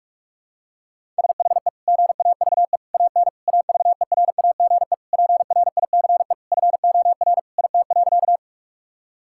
Morse code, heard as a keyed tone, about 41 wpm, 700 Hz. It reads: SHE GAVE AN AVERAGE PRICE FOR IT.
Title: Sentences from Top 1000 Words - Review Part 6 - 45wpm